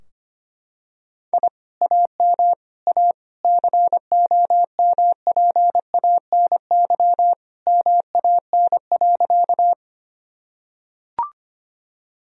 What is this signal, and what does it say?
Morse code, heard as a keyed tone, about 25 wpm, 700 Hz. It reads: I AM A COMPANY MAN.
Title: Sentences from Top 1000 Words - Review Part 3 - 25wpm